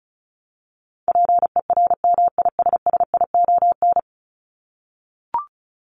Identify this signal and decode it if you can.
Morse code, heard as a keyed tone, about 34 wpm, 700 Hz. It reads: PERMISSION
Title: Top 1501-1750 Words 35wpm